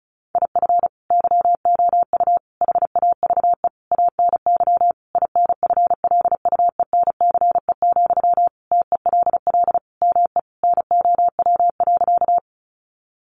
Morse code, heard as a keyed tone, about 34 wpm, 700 Hz. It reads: IF YOU HAVE ANY INFLUENCE, TELL ME NOW.
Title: Sentences from Top 1100 Words 35wpm